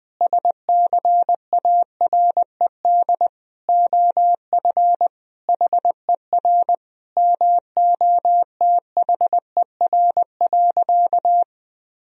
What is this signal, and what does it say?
Morse code, heard as a keyed tone, about 20 wpm, 700 Hz. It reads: SCARED OF HER MOTHER.
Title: Sentences from Top 1000 Words - Review Part 4 - 20wpm